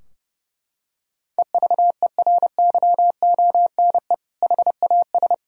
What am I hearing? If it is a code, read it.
Morse code, heard as a keyed tone, about 32 wpm, 700 Hz. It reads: EVERYONE HAS
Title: Sentences from Top 1000 Words - Review Part 4 - 30wpm